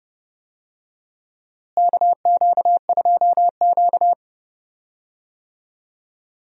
Morse code, heard as a keyed tone, about 30 wpm, 700 Hz. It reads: KQ2Q
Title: USA Call Signs - 4 Characters - Repeated 2x - 30wpm